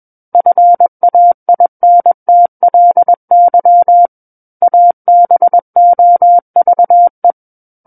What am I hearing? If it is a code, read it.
Morse code, heard as a keyed tone, about 21 wpm, 700 Hz. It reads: FAINTLY ABOVE
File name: Book One / 12-Chapter 12 - 21 WPM